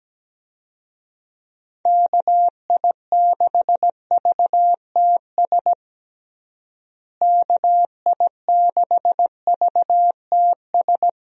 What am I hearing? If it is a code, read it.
Morse code, heard as a keyed tone, about 17 wpm, 700 Hz. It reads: KI6VTS KI6VTS
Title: USA Call Signs - 6 Characters - Repeated 2x - 17wpm